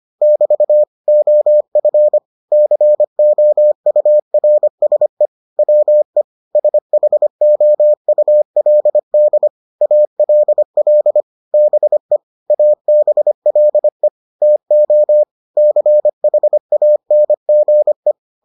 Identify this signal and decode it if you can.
Morse code, heard as a keyed tone, about 25 wpm, 600 Hz. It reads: <BT> OF COURSE WE SHOULD ALL BE ABLE TO CHANGE